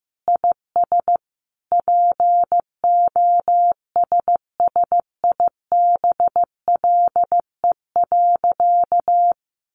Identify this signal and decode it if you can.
Morse code, heard as a keyed tone, about 15 wpm, 700 Hz. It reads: IS POSSIBLE.
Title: Sentences from Top 900 Words 15wpm